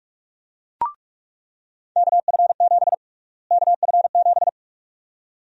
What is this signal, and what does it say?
Morse code, heard as a keyed tone, about 45 wpm, 700 Hz. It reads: KF7 KF7
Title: USA Call Sign Prefixes - Repeated 2x - 45wpm